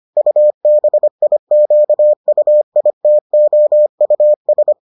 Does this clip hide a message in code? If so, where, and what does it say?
Yes: Morse code from 0.2 to 4.7 s, heard as a keyed tone, about 25 wpm, 600 Hz. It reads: UBIQUITOUS